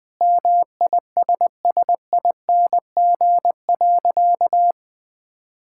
Morse code, heard as a keyed tone, about 21 wpm, 700 Hz. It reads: MISSING.
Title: Sentences with 4-Word Phrases - 20wpm